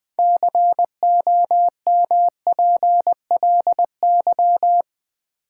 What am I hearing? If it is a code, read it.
Morse code, heard as a keyed tone, about 20 wpm, 700 Hz. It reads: COMPLY
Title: 6 Letter Words - First 3 Letters Given - 20wpm